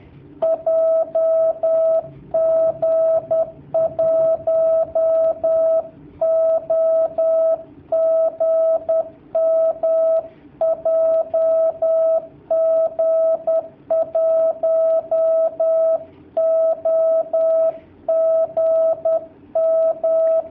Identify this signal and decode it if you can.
Morse code, heard as a keyed tone, about 10 wpm, 650 Hz. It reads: JG1OGM JG1OGM
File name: jh8cbh / 2019cw / sp10- w